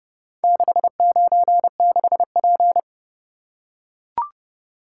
Morse code, heard as a keyed tone, about 29 wpm, 700 Hz. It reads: 696P